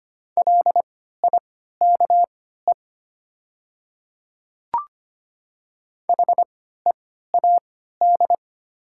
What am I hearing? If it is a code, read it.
Morse code, heard as a keyed tone, about 25 wpm, spaced as for 14 wpm, 700 Hz. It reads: LIKE HEAD
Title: 3x Character Spacing - Top 300 Words - 25wpm